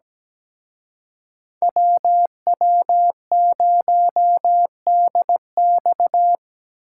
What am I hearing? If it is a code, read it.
Morse code, heard as a keyed tone, about 20 wpm, 700 Hz. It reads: WW0DX
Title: Speed Racing - USA 2x2 Call Signs 17wpm